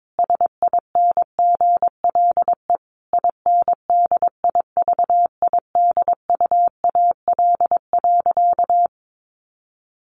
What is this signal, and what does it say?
Morse code, heard as a keyed tone, about 22 wpm, 700 Hz. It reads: SINGLE INDIVIDUAL.
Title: Sentences from Top 1500 Words 22wpm